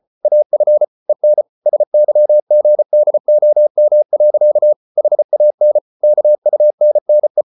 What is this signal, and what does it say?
Morse code, heard as a keyed tone, about 34 wpm, 600 Hz. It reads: AF EN SYGDOM. HAN KUNNE
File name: Ferie QRQ 34wpm